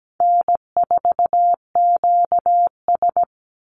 Morse code, heard as a keyed tone, about 17 wpm, 700 Hz. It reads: N4QS
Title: Easy Call Signs Repeated 2x - 17wpm